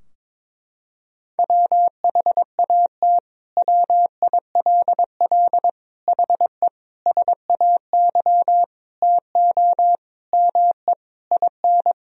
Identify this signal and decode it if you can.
Morse code, heard as a keyed tone, about 22 wpm, 700 Hz. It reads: WHAT WILL HE SAY TO ME IN
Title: Sentences from Top 1000 Words - Review Part 3 - 22wpm